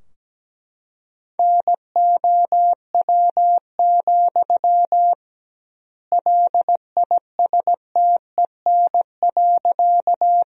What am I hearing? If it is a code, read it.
Morse code, heard as a keyed tone, about 17 wpm, 700 Hz. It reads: NOW, LISTEN.
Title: 2x Word-Spacing - Sentences from Top 400 Words - 17wpm